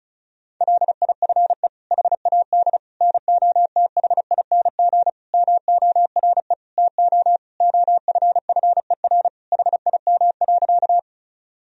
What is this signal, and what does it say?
Morse code, heard as a keyed tone, about 34 wpm, 700 Hz. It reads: LIFE HAD NOTHING MORE TO OFFER HIM.
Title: Sentences from Top 1000 Words - Review Part 4 - 35wpm